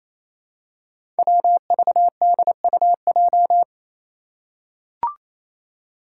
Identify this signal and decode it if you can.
Morse code, heard as a keyed tone, about 27 wpm, 700 Hz. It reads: WVDUJ
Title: Five Challenging Letter-Number Combinations 28wpm